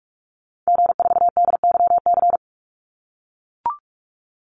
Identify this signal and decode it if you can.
Morse code, heard as a keyed tone, about 44 wpm, 700 Hz. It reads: G4DYC
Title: Call Signs - 45wpm